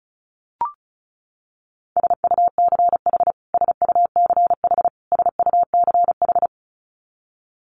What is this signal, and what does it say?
Morse code, heard as a keyed tone, about 34 wpm, 700 Hz. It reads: SUCH SUCH SUCH
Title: Top 500 Words Repeated 35wpm